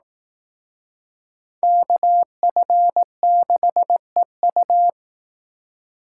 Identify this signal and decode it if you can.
Morse code, heard as a keyed tone, about 19 wpm, 700 Hz. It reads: KF6EU
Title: Speed Racing - USA 2x2 Call Signs 15wpm